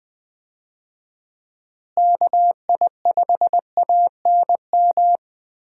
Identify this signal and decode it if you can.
Morse code, heard as a keyed tone, about 20 wpm, 700 Hz. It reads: KI5ANM
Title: USA Call Signs - 6 Characters - Repeated 3x - 20wpm